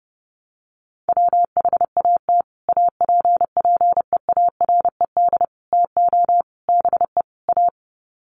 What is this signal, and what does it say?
Morse code, heard as a keyed tone, about 30 wpm, 700 Hz. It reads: WHAT APPEARED TO BE A
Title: Sets of 5 Words Encore2 30wpm